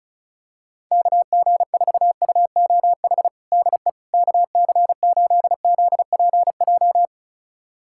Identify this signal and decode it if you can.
Morse code, heard as a keyed tone, about 36 wpm, 700 Hz. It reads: KG4UOH DE KC8ZPJ
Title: USA Callsign DE Callsign - 6 Characters - 35wpm